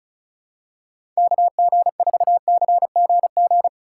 Morse code, heard as a keyed tone, about 36 wpm, 700 Hz. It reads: KG4CGG
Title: USA Call Signs - 6 Characters - 2x Character Spacing - 35wpm